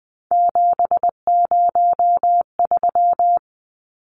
Morse code, heard as a keyed tone, about 20 wpm, 700 Hz. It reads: 703
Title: Numbers - 3-Digit 20wpm